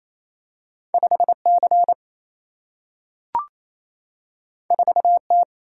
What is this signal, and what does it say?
Morse code, heard as a keyed tone, about 28 wpm, 700 Hz. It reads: 5C 4T
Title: Single-Letter Single-Number 28wpm